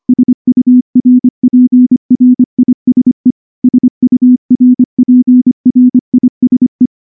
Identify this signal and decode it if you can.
Morse code, heard as a keyed tone, about 24 wpm, 250 Hz. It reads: SURPRISE SURPRISE
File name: morse-training-25WPM